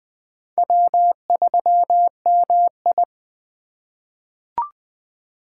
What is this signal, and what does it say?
Morse code, heard as a keyed tone, about 20 wpm, 700 Hz. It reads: W3MI
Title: USA Call Signs - 4 Characters - 4x Character Spacing - 20wpm